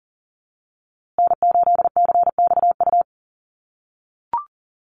Morse code, heard as a keyed tone, about 40 wpm, 700 Hz. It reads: N8CXU
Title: USA Call Signs Repeated 2x 40wpm